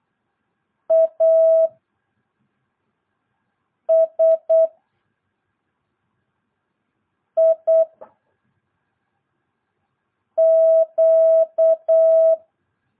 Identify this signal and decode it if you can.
Morse code, heard as a keyed tone, about 8 wpm, 660 Hz. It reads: A S I Q
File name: jh8cbh / 2022cw